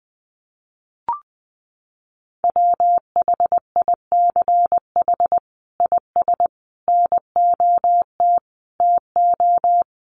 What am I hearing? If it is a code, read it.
Morse code, heard as a keyed tone, about 20 wpm, 700 Hz. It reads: WHICH IS NOT TO
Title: Sentences from Top 1400 Words 20wpm